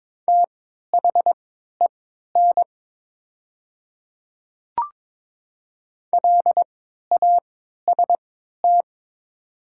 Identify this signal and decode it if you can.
Morse code, heard as a keyed tone, about 22 wpm, spaced as for 12 wpm, 700 Hz. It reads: THEN LAST
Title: 3x Character Spacing - Top 300 Words - 22wpm